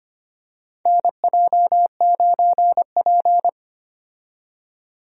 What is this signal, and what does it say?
Morse code, heard as a keyed tone, about 25 wpm, 700 Hz. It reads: NJ9P
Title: USA Call Signs - 4 Characters - 25wpm